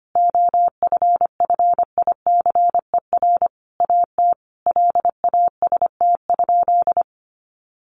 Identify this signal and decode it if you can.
Morse code, heard as a keyed tone, about 25 wpm, 700 Hz. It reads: OFFICER AT LAST?
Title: Sentences from Top 1300 Words 25wpm